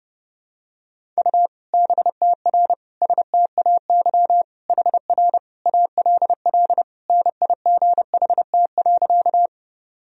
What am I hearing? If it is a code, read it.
Morse code, heard as a keyed tone, about 29 wpm, 700 Hz. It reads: U BTR STAY HR ALL NIGHT.
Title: Sentences from Top 500 Words and Abbreviations 30wpm